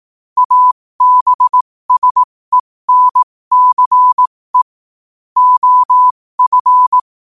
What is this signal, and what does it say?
Morse code, heard as a keyed tone, about 18 wpm, spaced as for 15 wpm, 1000 Hz. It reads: ABSENCE OF
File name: qotdfiles / 15wpm